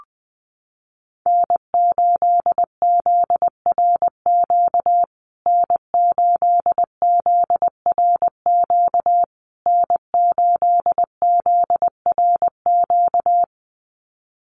Morse code, heard as a keyed tone, about 19 wpm, 700 Hz. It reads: N8ZRQ N8ZRQ N8ZRQ
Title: USA Call Signs Repeated 3x 20wpm